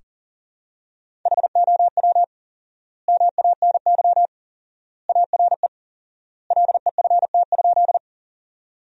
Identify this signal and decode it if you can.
Morse code, heard as a keyed tone, about 40 wpm, 700 Hz. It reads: HOW MANY ARE LEFT?
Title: 4x Word-Spacing - Sentences from Top 300 Words - 40wpm